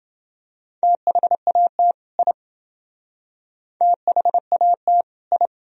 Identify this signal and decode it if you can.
Morse code, heard as a keyed tone, about 28 wpm, 700 Hz. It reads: THAT I THAT I